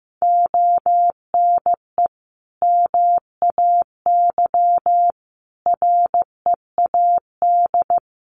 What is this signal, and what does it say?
Morse code, heard as a keyed tone, about 14 wpm, 700 Hz. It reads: ONE MAY READ
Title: Sentences from Top 1100 Words 15wpm